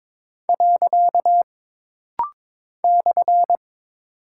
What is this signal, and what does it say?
Morse code, heard as a keyed tone, about 22 wpm, 700 Hz. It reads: . /
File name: practice / 22wpm / Single Punctuation-Prosign - Rapid-Fire - 22wpm